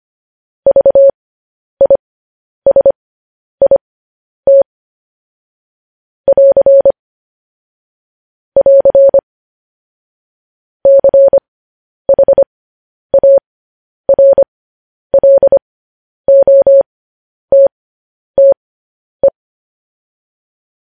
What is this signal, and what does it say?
Morse code, heard as a keyed tone, about 25 wpm, spaced as for 10 wpm, 550 Hz. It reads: VISIT <AR> <AR> CHARLOTTE
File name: podcast / News headlines-1x-25x10